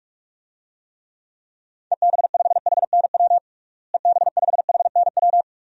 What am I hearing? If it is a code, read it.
Morse code, heard as a keyed tone, about 46 wpm, 700 Hz. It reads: EB5HNW EB5HNW
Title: DXCC-100 Call Signs - 6 Characters - Repeated 2x - 45wpm